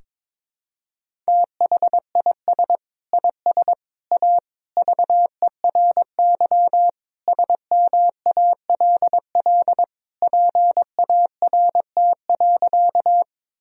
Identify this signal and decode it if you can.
Morse code, heard as a keyed tone, about 22 wpm, 700 Hz. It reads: THIS IS A VERY SMALL PART.
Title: Sentences from Top 1000 Words - Review Part 1 - 22wpm